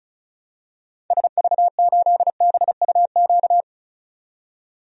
Morse code, heard as a keyed tone, about 36 wpm, 690 Hz. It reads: SV8BUQ
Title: DXCC-100 Call Signs - 6 Characters - 35wpm